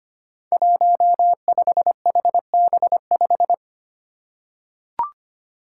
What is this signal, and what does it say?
Morse code, heard as a keyed tone, about 25 wpm, 700 Hz. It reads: 15HB5
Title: Five Challenging Letter-Number Combinations 25wpm